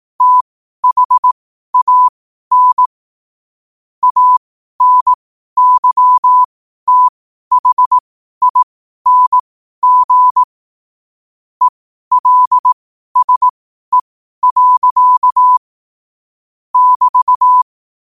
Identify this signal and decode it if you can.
Morse code, heard as a keyed tone, about 18 wpm, spaced as for 12 wpm, 1000 Hz. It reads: THAN ANYTHING ELSE. <BT>
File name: qotdfiles / 12wpm